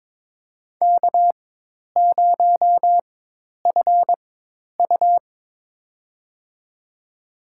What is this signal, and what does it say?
Morse code, heard as a keyed tone, about 22 wpm, spaced as for 10 wpm, 700 Hz. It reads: K0FU
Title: USA Call Signs - 4 Characters - 4x Character Spacing - 22wpm